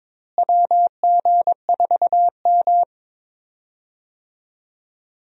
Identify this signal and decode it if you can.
Morse code, heard as a keyed tone, about 21 wpm, 700 Hz. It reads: WG4M
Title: USA Call Signs - 4 Characters - Repeated 2x - 22wpm